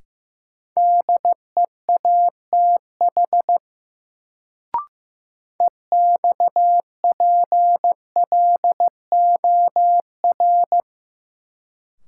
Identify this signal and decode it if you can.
Morse code, heard as a keyed tone, about 16 wpm, 700 Hz. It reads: DEATH EXPLOR
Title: Sentences from Top 1100 Words - New Word Drill - 15wpm